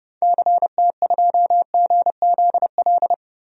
Morse code, heard as a keyed tone, about 30 wpm, 700 Hz. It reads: CT2GZL
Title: DXCC-100 Call Signs - 6 Characters - Repeated 2x - 30wpm